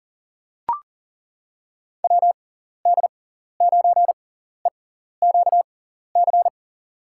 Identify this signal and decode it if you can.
Morse code, heard as a keyed tone, about 39 wpm, spaced as for 14 wpm, 700 Hz. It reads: WD9EQC
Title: USA Call Signs - 6 Characters - 6x Character Spacing - 40wpm